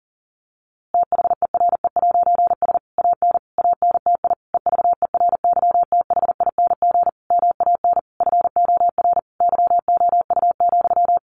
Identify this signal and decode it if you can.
Morse code, heard as a keyed tone, about 40 wpm, 700 Hz. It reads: THERE'S AN ANTI EVERYTHING MAN FOR YOU,
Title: A Sound Of Thunder 40wpm